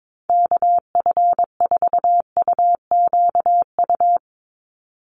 Morse code, heard as a keyed tone, about 22 wpm, 700 Hz. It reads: KF4UQU
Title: USA Call Signs 22wpm